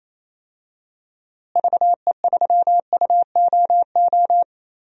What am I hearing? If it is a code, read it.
Morse code, heard as a keyed tone, about 28 wpm, 700 Hz. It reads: VE3UOO
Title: DXCC-100 Call Signs - 6 Characters - Repeated 3x - 28wpm